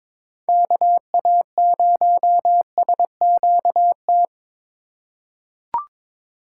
Morse code, heard as a keyed tone, about 22 wpm, 700 Hz. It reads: KA0SQT
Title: USA Call Signs - 6 Characters - 3x Character Spacing - 22wpm